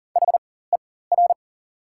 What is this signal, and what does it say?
Morse code, heard as a keyed tone, about 40 wpm, spaced as for 19 wpm, 700 Hz. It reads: HER
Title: 4x Character Spacing - Top 200 Words - 40wpm